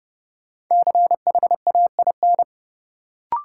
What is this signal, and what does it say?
Morse code, heard as a keyed tone, about 30 wpm, 700 Hz. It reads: CHAIN
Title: Sentences from Top 1500 Words - New Word Drill - 30wpm